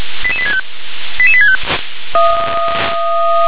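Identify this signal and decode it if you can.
Five-tone selective call (ZVEI), heard as quick runs of five tones, tones of about 70 ms. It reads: 88865 88865
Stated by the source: ZVEI five-tone selective call received on SDR.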